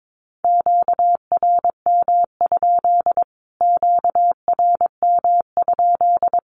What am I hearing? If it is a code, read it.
Morse code, heard as a keyed tone, about 22 wpm, 700 Hz. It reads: QRM? QRM?
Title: Q-Codes Repeated 3x 22wpm